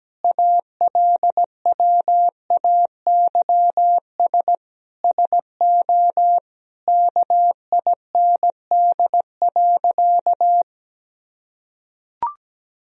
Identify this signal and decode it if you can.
Morse code, heard as a keyed tone, about 17 wpm, 700 Hz. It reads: ALWAYS SO KIND.
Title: Sentences from Top 1000 Words - Review Part 3 - 17wpm